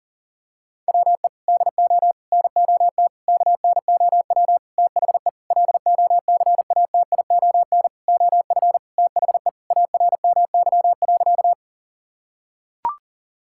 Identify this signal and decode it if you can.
Morse code, heard as a keyed tone, about 40 wpm, 700 Hz. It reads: WE DO NOT KNOW THE LOCATION OF THE ARMY.
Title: Sentences from Top 1000 Words - Review Part 6 - 40wpm